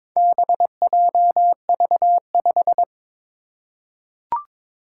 Morse code, heard as a keyed tone, about 22 wpm, 700 Hz. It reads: BJV5
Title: Four Challenging Letter-Number Combinations 22wpm